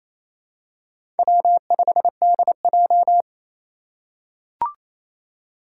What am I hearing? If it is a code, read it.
Morse code, heard as a keyed tone, about 28 wpm, 700 Hz. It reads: W5DJ
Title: Four Challenging Letter-Number Combinations 28wpm